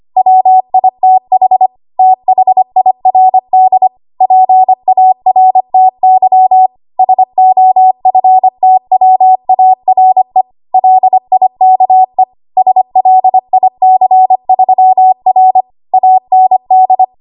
Morse code, heard as a keyed tone, about 25 wpm, 750 Hz. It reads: WITH THIRD PARTY SOFTWARE LIKE SLIC3R AND
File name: Archive / 25 WPM / 25WPM